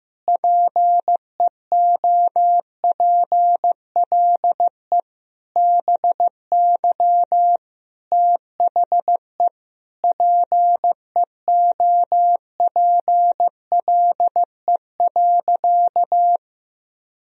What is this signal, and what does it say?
Morse code, heard as a keyed tone, about 15 wpm, 700 Hz. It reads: PEOPLE BY THE PEOPLE.
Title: Sentences from Top 1000 Words - Review Part 3 - 15wpm